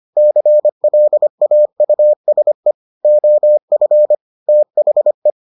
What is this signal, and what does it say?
Morse code, heard as a keyed tone, about 25 wpm, 600 Hz. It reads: CLAUSE OF THE